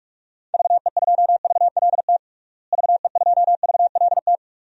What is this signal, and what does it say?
Morse code, heard as a keyed tone, about 46 wpm, 700 Hz. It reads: VE2VLT VE2VLT
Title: DXCC-100 Call Signs - 6 Characters - Repeated 2x - 45wpm